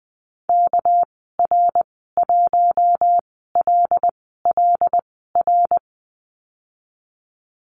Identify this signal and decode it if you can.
Morse code, heard as a keyed tone, about 20 wpm, spaced as for 15 wpm, 700 Hz. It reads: KR1LLR
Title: 231 Lesson 37g - Letter x - Call Signs 20wpm